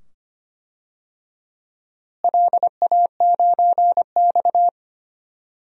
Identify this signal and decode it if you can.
Morse code, heard as a keyed tone, about 25 wpm, 700 Hz. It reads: LA9X
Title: DXCC-100 Call Signs - 4 Characters - Repeated 2x - 25wpm